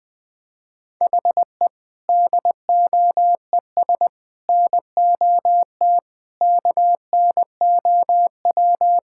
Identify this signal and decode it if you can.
Morse code, heard as a keyed tone, about 20 wpm, 700 Hz. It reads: HE DOES NOT KNOW
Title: Sentences with 4-Word Phrases - Spoken Beginning - 20wpm